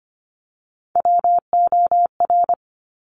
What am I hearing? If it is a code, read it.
Morse code, heard as a keyed tone, about 25 wpm, 700 Hz. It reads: WOR
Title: Top Three Letter Combinations in Words 25wpm